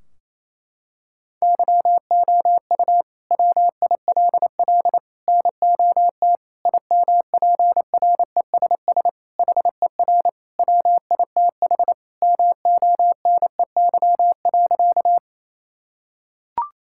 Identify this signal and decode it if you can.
Morse code, heard as a keyed tone, about 28 wpm, 700 Hz. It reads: YOU WILL NOT IMPRESS HER WITH MONEY.
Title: Sentences from Top 1600 Words 28wpm